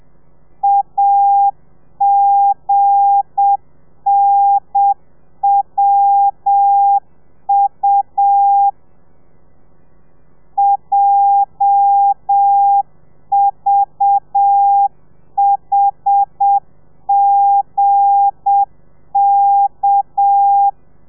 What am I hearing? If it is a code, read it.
Morse code, heard as a keyed tone, about 7 wpm, 800 Hz. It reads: AGNWU JVHGK
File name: telegrafia / i2tfj / Lez8 30